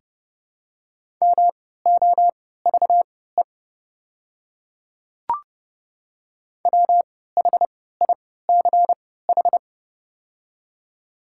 Morse code, heard as a keyed tone, about 32 wpm, spaced as for 17 wpm, 700 Hz. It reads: MOVE WHICH
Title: 3x Character Spacing - Top 300 Words - 30wpm